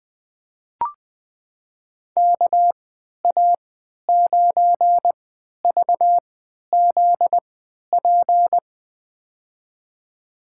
Morse code, heard as a keyed tone, about 20 wpm, spaced as for 11 wpm, 700 Hz. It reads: KA9VZP
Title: USA Call Signs - 6 Characters - 3x Character Spacing - 20wpm